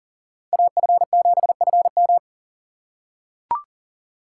Morse code, heard as a keyed tone, about 40 wpm, 700 Hz. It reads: AF7FM
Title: USA Call Signs - 5 Characters - 6x Character Spacing - 40wpm